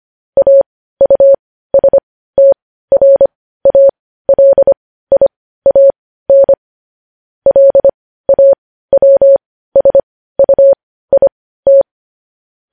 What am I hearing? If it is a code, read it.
Morse code, heard as a keyed tone, about 25 wpm, spaced as for 15 wpm, 550 Hz. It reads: AUSTRALIAN LAWSUIT
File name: podcast / News headlines-1x-25x15